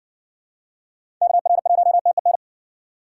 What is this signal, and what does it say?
Morse code, heard as a keyed tone, about 60 wpm, 700 Hz. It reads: KR1TEN